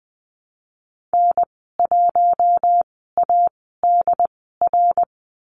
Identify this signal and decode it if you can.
Morse code, heard as a keyed tone, about 20 wpm, spaced as for 15 wpm, 700 Hz. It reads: N1ADR
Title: 069 Lesson 12g - Letter d - Call Signs 20wpm